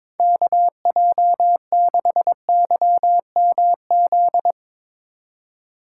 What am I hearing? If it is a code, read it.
Morse code, heard as a keyed tone, about 21 wpm, 700 Hz. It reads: KJ6YMZ
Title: USA Call Signs - 3x Character Spacing - 22wpm